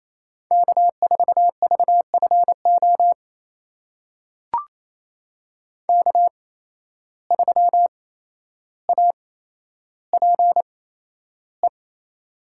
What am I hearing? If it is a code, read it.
Morse code, heard as a keyed tone, about 27 wpm, spaced as for 12 wpm, 700 Hz. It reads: K4VFO K3APE
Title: USA Call Signs - 5 Characters - 8x Character Spacing - 28wpm